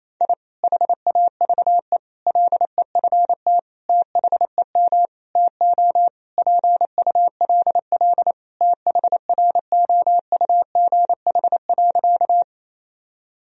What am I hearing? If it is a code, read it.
Morse code, heard as a keyed tone, about 28 wpm, 690 Hz. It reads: I HAVE LEFT THEM TO PULL THROUGH.
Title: Sentences from Top 1000 Words - Review Part 2 - 28wpm